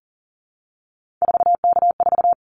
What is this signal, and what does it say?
Morse code, heard as a keyed tone, about 40 wpm, 700 Hz. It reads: 4K4
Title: Callsign Prefixes for Top 100 DXCC - 40wpm